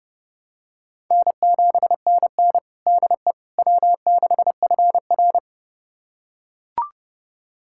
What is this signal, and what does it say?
Morse code, heard as a keyed tone, about 30 wpm, 700 Hz. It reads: N7NN DE W6FR
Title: USA Callsign DE Callsign - 4 Characters - 30wpm